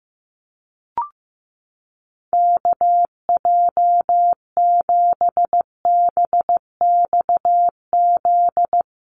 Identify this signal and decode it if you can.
Morse code, heard as a keyed tone, about 15 wpm, 700 Hz. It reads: KJ7BXZ
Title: USA Call Signs Repeated 3x 15wpm